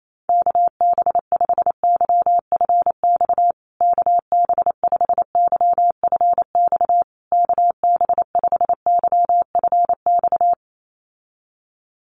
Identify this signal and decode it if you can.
Morse code, heard as a keyed tone, about 28 wpm, 690 Hz. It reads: KB5YFX KB5YFX KB5YFX
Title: USA Call Signs Repeated 3x 28wpm